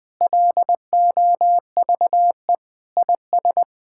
Morse code, heard as a keyed tone, about 20 wpm, 700 Hz. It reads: LOVE IS
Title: Sentences with 5-Word Phrases - Spoken Beginning - 20wpm